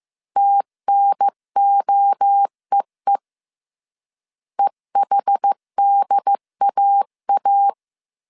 Morse code, heard as a keyed tone, about 15 wpm, 800 Hz. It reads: TNOEE EHDAA
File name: CW / DET / 1 3 1500C S C 14wpm 2458